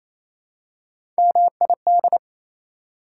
Morse code, heard as a keyed tone, about 28 wpm, 700 Hz. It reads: MID
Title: Prefixes and Suffixes 28wpm